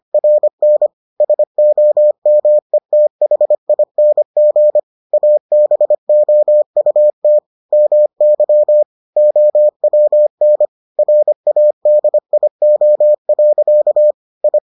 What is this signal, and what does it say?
Morse code, heard as a keyed tone, about 25 wpm, 600 Hz. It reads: RN SOMETHING ABOUT MY OWN RADIO. I